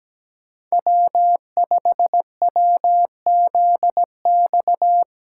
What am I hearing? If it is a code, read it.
Morse code, heard as a keyed tone, about 17 wpm, 700 Hz. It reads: W5WZX
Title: USA Callsign DE Callsign 17wpm